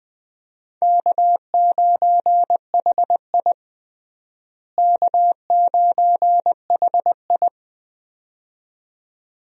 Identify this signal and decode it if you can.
Morse code, heard as a keyed tone, about 20 wpm, 700 Hz. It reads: K9HI K9HI
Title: USA Call Signs - 4 Characters - Repeated 2x - 20wpm